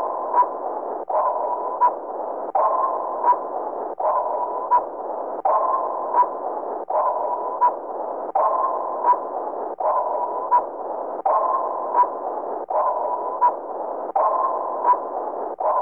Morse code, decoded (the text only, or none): none